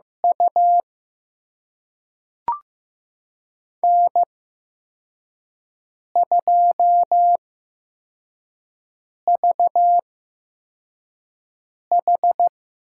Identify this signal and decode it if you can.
Morse code, heard as a keyed tone, about 15 wpm, spaced as for 4 wpm, 700 Hz. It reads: U N2VH
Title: USA Call Signs - 4 Characters - 8x Character Spacing - 15wpm